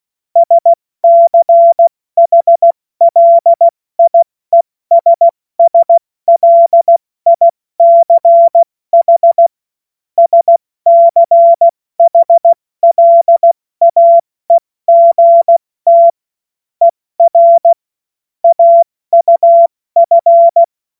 Morse code, heard as a keyed tone, about 16 wpm, spaced as for 14 wpm, 680 Hz. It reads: SCHLIESSLICH SCHLAEGT ER AUF
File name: Morsekurs / Witze / Witz 07. Dezember 2021, 70 F80